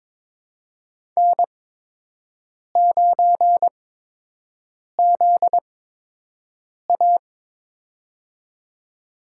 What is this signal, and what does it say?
Morse code, heard as a keyed tone, about 23 wpm, spaced as for 6 wpm, 700 Hz. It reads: N9ZA
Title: USA Call Signs - 4 Characters - 8x Character Spacing - 22wpm